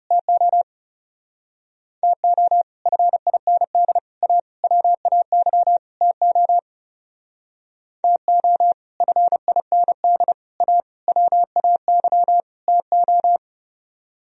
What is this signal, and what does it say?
Morse code, heard as a keyed tone, about 37 wpm, 700 Hz. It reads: TO TO FIND A WAY TO TO FIND A WAY TO
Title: Speed-Racing - Sets of 5 Words 30wpm